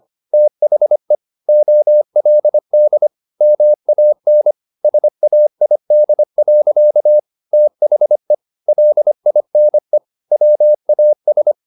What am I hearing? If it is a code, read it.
Morse code, heard as a keyed tone, about 25 wpm, 600 Hz. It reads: THE OLD MAN SAID. THE LINE WAS
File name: the old man and the sea part 1 25wpm